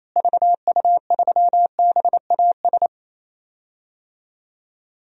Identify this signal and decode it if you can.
Morse code, heard as a keyed tone, about 28 wpm, 700 Hz. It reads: VU3BAS
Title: DXCC-100 Call Signs - 6 Characters - Repeated 2x - 28wpm